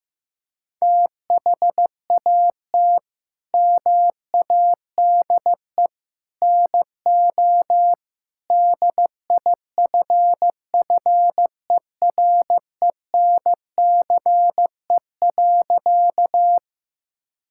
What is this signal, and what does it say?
Morse code, heard as a keyed tone, about 15 wpm, 700 Hz. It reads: THAT MADE NO DIFFERENCE.
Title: Sentences from Top 1000 Words - Review Part 5 - 15wpm